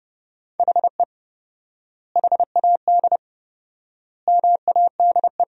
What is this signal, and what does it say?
Morse code, heard as a keyed tone, about 30 wpm, 700 Hz. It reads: HE HAD MADE
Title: 4x Word-Spacing - Sentences from Top 500 Words - 30wpm